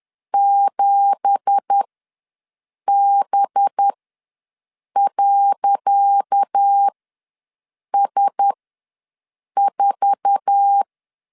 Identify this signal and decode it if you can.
Morse code, heard as a keyed tone, about 10 wpm, spaced as for 6 wpm, 800 Hz. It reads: 7B.S4
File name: LEZIONI / 10 SIGNS / 10C DOT ws 10wpm 4213